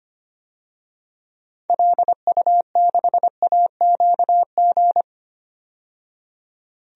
Morse code, heard as a keyed tone, about 25 wpm, 700 Hz. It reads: LU6AQG
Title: DXCC-100 Call Signs - 6 Characters - 25wpm